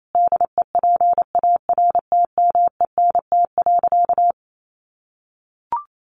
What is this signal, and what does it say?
Morse code, heard as a keyed tone, about 28 wpm, 700 Hz. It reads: DEPARTMENT.
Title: Sentences from Top 1200 Words 28wpm